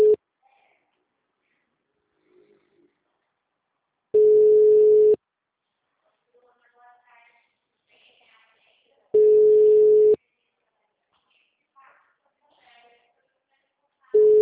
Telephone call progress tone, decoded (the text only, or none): ringback tone